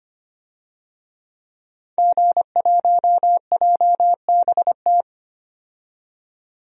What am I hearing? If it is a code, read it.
Morse code, heard as a keyed tone, about 25 wpm, 700 Hz. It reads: G1JBT
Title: DXCC-100 Call Signs - 5 Characters - 25wpm